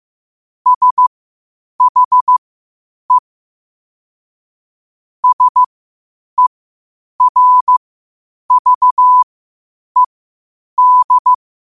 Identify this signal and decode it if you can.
Morse code, heard as a keyed tone, about 15 wpm, spaced as for 8 wpm, 1000 Hz. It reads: SHE SERVED
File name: qotdfiles / 8wpm